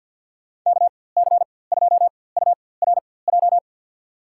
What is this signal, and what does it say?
Morse code, heard as a keyed tone, about 50 wpm, spaced as for 23 wpm, 700 Hz. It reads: KC2URJ
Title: USA Call Signs - 6 Characters - 4x Character Spacing - 50wpm